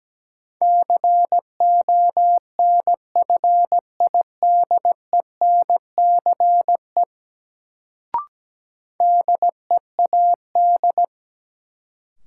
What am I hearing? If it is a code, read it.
Morse code, heard as a keyed tone, about 17 wpm, 700 Hz. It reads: CONFIDENCE DEAD
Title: Sentences from Top 1500 Words - New Word Drill Alt - 17wpm